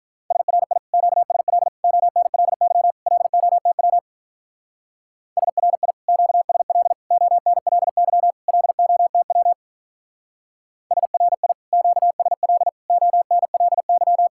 Morse code, heard as a keyed tone, about 43 wpm, 700 Hz. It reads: SRI QSL ONLY LOTW SRI QSL ONLY LOTW SRI QSL ONLY
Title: Speed-Racing Common QSO Phrases - 35wpm